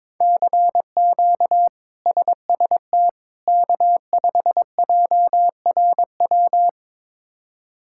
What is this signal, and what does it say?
Morse code, heard as a keyed tone, about 22 wpm, 700 Hz. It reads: CQ SST K5JRW
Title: K1USN SST QSO Practice - 22wpm